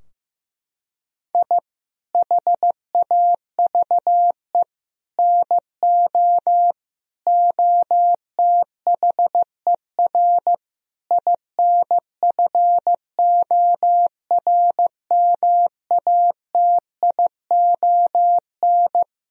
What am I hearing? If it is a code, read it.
Morse code, heard as a keyed tone, about 15 wpm, 700 Hz. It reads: I HAVE NO OTHER INFORMATION
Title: Sentences from Top 1000 Words - Review Part 3 - 15wpm